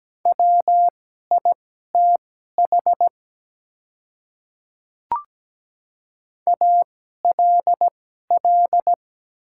Morse code, heard as a keyed tone, about 17 wpm, spaced as for 12 wpm, 700 Hz. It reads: WITH ALL
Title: 2x Character Spacing - Top 100 Words - 17wpm